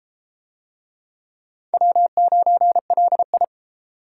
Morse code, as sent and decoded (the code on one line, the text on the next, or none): .-- ----. .-.. ..
W9LI